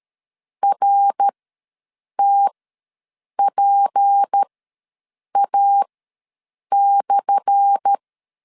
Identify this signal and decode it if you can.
Morse code, heard as a keyed tone, about 13 wpm, spaced as for 7 wpm, 800 Hz. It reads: RTPA/
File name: LEZIONI / 10 SIGNS / 10A SLASH ws 12wpm 3443